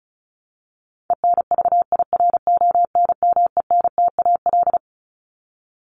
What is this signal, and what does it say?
Morse code, heard as a keyed tone, about 34 wpm, 700 Hz. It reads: ENVIRONMENTAL
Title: Top 1001-1250 Words 35wpm